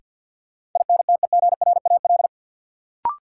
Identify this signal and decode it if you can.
Morse code, heard as a keyed tone, about 52 wpm, 700 Hz. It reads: INTEGRAL
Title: Sets of 5 Words - New Word Drill - 50wpm